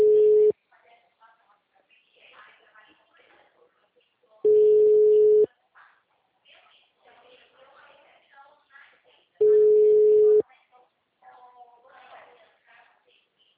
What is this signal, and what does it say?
Ringback tone (Europe, 425 Hz), with speech.